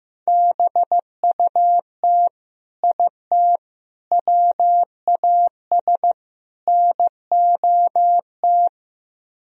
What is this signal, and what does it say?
Morse code, heard as a keyed tone, about 17 wpm, 700 Hz. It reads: BUT IT WAS NOT
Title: Speed-Racing - Sets of 4 Words 15wpm